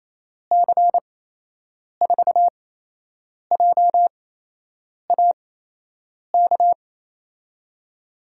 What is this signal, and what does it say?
Morse code, heard as a keyed tone, about 28 wpm, spaced as for 8 wpm, 700 Hz. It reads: C4JAK
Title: USA Call Signs - 6 Characters - 8x Character Spacing - 28wpm